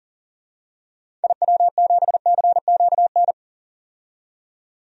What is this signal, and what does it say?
Morse code, heard as a keyed tone, about 40 wpm, 700 Hz. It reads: IW7CQN
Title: DXCC-100 Call Signs - 6 Characters - 40wpm